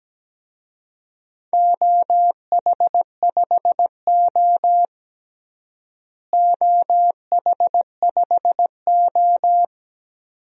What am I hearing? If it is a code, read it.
Morse code, heard as a keyed tone, about 17 wpm, 700 Hz. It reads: OH5O OH5O
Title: DXCC-100 Call Signs - 4 Characters - Repeated 2x - 17wpm